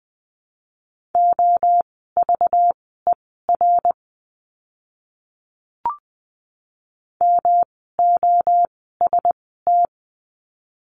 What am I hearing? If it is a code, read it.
Morse code, heard as a keyed tone, about 20 wpm, spaced as for 14 wpm, 700 Hz. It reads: OVER MOST
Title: 237 Lesson 38f - Letter q - Words 20wpm